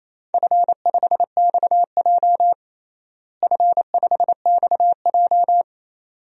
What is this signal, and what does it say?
Morse code, heard as a keyed tone, about 28 wpm, 700 Hz. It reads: F5XJ F5XJ
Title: DXCC-100 Call Signs - 4 Characters - Repeated 2x - 28wpm